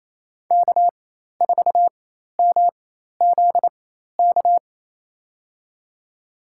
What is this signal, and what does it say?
Morse code, heard as a keyed tone, about 28 wpm, spaced as for 13 wpm, 700 Hz. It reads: K4MZK
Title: USA Call Signs - 5 Characters - 4x Character Spacing - 28wpm